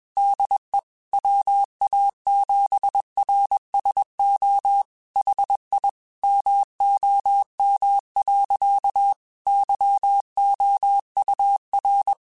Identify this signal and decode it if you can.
Morse code, heard as a keyed tone, about 21 wpm, 800 Hz. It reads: DE WA7RSO HI MOM. YOUR
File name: QSOs / QSOs-mp3s / WA7QOX 1@20